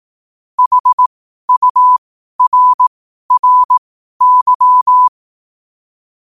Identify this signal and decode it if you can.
Morse code, heard as a keyed tone, about 18 wpm, spaced as for 12 wpm, 1000 Hz. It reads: HURRY
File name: qotdfiles / 12wpm